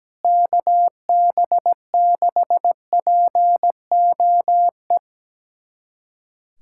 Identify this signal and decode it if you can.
Morse code, heard as a keyed tone, about 17 wpm, 700 Hz. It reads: KB6POE
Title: USA Callsign DE Callsign - 6 Characters - 17wpm